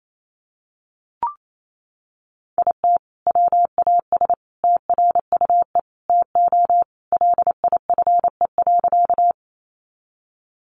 Morse code, heard as a keyed tone, about 28 wpm, 700 Hz. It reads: IT WAS TRUE TO LIFE.
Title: Sentences from Top 400 Words 28wpm